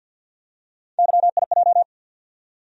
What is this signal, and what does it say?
Morse code, heard as a keyed tone, about 50 wpm, 700 Hz. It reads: YIJ
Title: USA Call Sign Postfixes - 3 Letters 50wpm